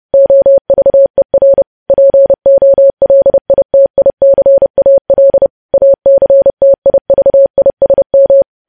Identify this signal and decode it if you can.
Morse code, heard as a keyed tone, about 30 wpm, 550 Hz. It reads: OVER POLITICAL ACTIVISM
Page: news headlines in morse code at 30 wpm